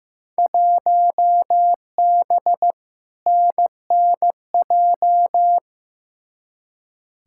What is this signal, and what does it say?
Morse code, heard as a keyed tone, about 15 wpm, 700 Hz. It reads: 1B NNJ
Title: Field Day Exchanges - 15wpm